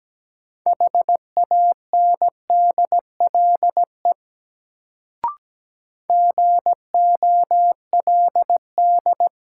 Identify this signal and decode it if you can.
Morse code, heard as a keyed tone, about 17 wpm, 700 Hz. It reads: HANDLE GOLD
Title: Sentences from Top 1100 Words - New Word Drill Alt - 17wpm